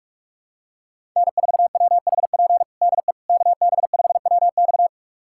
Morse code, heard as a keyed tone, about 46 wpm, 700 Hz. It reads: N4WHP DE KB5WX
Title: USA Callsign DE Callsign - 5 Characters - 45wpm